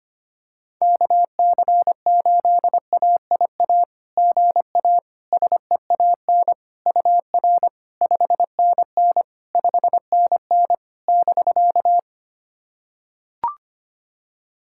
Morse code, heard as a keyed tone, about 25 wpm, 700 Hz. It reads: KC8AIA GA SEAN UR 5NN 5NN <BK>
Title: SOTA QSO Practice 25wpm